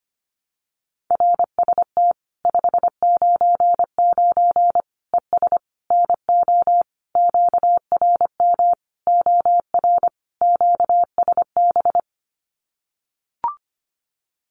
Morse code, heard as a keyed tone, about 25 wpm, 700 Hz. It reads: RST 599 ES NO QRM OR QSB
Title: Common QSO Phrases 25wpm